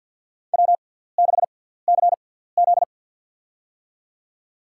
Morse code, heard as a keyed tone, about 52 wpm, spaced as for 17 wpm, 700 Hz. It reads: W6CZ
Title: USA Call Signs - 4 Characters - 6x Character Spacing - 50wpm